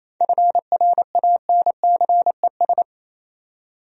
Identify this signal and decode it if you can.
Morse code, heard as a keyed tone, about 28 wpm, 700 Hz. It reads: FRANCES
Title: USA Common Names - 28wpm